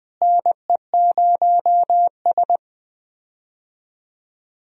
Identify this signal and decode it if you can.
Morse code, heard as a keyed tone, about 20 wpm, 700 Hz. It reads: NE0S
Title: USA Call Signs - 4 Characters - Repeated 2x - 20wpm